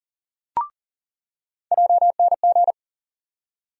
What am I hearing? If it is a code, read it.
Morse code, heard as a keyed tone, about 40 wpm, 700 Hz. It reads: JNG
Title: USA Call Sign Postfixes - 3 Letters 40wpm